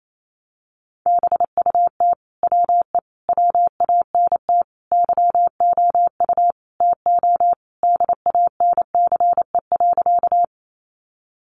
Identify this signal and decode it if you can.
Morse code, heard as a keyed tone, about 28 wpm, 700 Hz. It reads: BUT WE WANT YOU TO DANCE.
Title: Sentences from Top 700 Words 28wpm